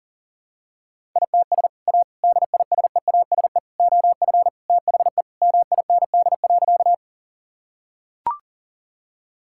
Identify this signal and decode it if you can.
Morse code, heard as a keyed tone, about 37 wpm, 700 Hz. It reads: ITS A DISEASE OF THE MIND.
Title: Sentences from Top 1000 Words - Review Part 6 - 40wpm